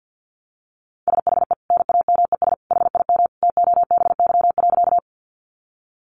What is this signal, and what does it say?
Morse code, heard as a keyed tone, about 50 wpm, 700 Hz. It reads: SHE NAMES HIM TOBY.
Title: Easy Phrases 50wpm